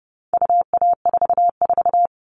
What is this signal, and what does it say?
Morse code, heard as a keyed tone, about 30 wpm, 700 Hz. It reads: UA44
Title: All Cut Numbers - 4-Digit 30wpm